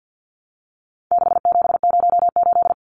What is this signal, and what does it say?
Morse code, heard as a keyed tone, about 50 wpm, 700 Hz. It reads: 6708
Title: Numbers - 4-Digit 50wpm